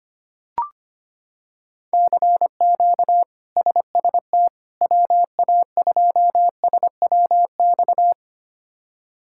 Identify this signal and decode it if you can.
Morse code, heard as a keyed tone, about 25 wpm, 700 Hz. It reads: CQ SST WA2SWX
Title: K1USN SST QSO Practice - 25wpm